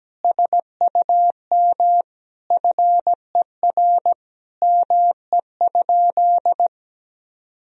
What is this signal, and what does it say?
Morse code, heard as a keyed tone, about 17 wpm, 700 Hz. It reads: SUM FER ME?
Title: Sentences from Top 200 Words and Abbreviations 17wpm